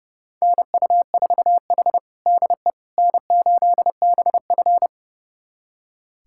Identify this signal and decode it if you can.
Morse code, heard as a keyed tone, about 30 wpm, 700 Hz. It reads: NU4H DE N8BF
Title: USA Callsign DE Callsign - 4 Characters - 30wpm